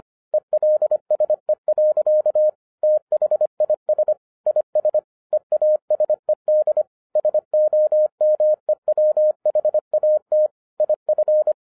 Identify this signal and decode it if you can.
Morse code, heard as a keyed tone, about 25 wpm, 610 Hz. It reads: ELSE. THIS IS EASED SOMEWHAT IF
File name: 2012-01-mp3 / n14-FUN WITH NUMBERS-25wpm